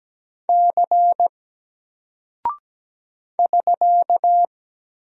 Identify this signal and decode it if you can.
Morse code, heard as a keyed tone, about 17 wpm, 700 Hz. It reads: C <SK>
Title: Single Character (Letter:Number:Punctuation:Prosign) 17wpm